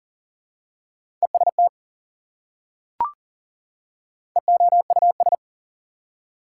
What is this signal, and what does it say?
Morse code, heard as a keyed tone, about 40 wpm, 700 Hz. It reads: EST EOUS
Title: Prefixes and Suffixes 40wpm